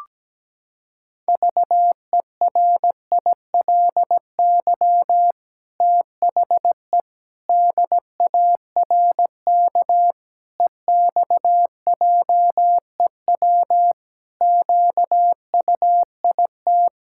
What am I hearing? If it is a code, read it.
Morse code, heard as a keyed tone, about 17 wpm, 700 Hz. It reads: VERILY THE DARK EXJEW QUIT
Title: Pangram Challenge v2 - 17wpm